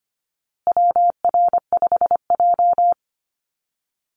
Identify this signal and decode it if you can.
Morse code, heard as a keyed tone, about 25 wpm, 700 Hz. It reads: WR5J
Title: Call Signs Repeated 2x - 25wpm